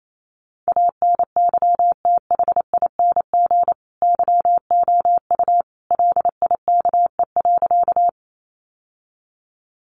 Morse code, heard as a keyed tone, about 28 wpm, 700 Hz. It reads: ANYTHING YOU LIKE.
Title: Sentences from Top 1000 Words 28wpm